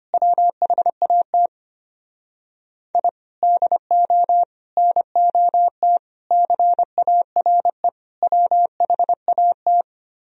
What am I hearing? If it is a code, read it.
Morse code, heard as a keyed tone, about 33 wpm, 700 Hz. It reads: WHAT I DO NOT CARE WHAT